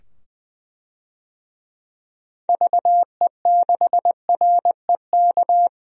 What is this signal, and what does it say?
Morse code, heard as a keyed tone, about 20 wpm, 700 Hz. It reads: VE6REK
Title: DXCC-100 Call Signs - 6 Characters - 20wpm